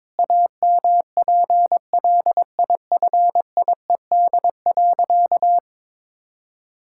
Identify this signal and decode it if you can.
Morse code, heard as a keyed tone, about 22 wpm, 700 Hz. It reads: AMPLIFIED.
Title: Pangram Challenge v2 - 22wpm